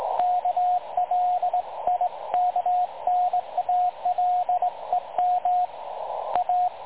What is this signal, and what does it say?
Morse code, heard as a keyed tone, about 19 wpm, 720 Hz, with radio noise. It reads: KLIKNALEM A
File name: dlazaawansowanych / nagraniazpasma / websdr recording 3565.0kHz